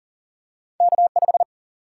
Morse code, heard as a keyed tone, about 40 wpm, 700 Hz. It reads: K 5
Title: Single-Letter Single-Number 40wpm